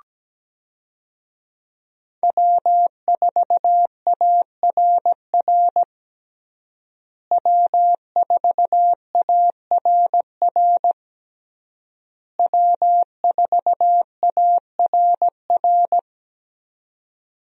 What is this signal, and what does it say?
Morse code, heard as a keyed tone, about 17 wpm, 700 Hz. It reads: W4ARR W4ARR W4ARR
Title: USA Call Signs - 5 Characters - Repeated 3x - 17wpm